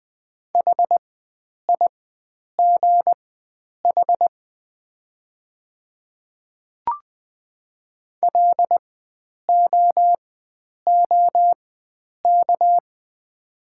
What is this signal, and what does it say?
Morse code, heard as a keyed tone, about 20 wpm, spaced as for 9 wpm, 690 Hz. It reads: HIGH LOOK
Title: 4x Character Spacing - Top 500 Words - 20wpm